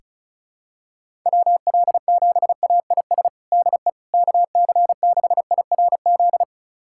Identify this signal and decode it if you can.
Morse code, heard as a keyed tone, about 36 wpm, 700 Hz. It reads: WL7AIS DE KC6IRZ
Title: USA Callsign DE Callsign 35wpm